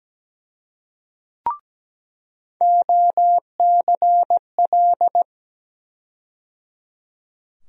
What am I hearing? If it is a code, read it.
Morse code, heard as a keyed tone, about 17 wpm, 700 Hz. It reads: OCL
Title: USA Call Sign Postfixes - 3 Letters 17wpm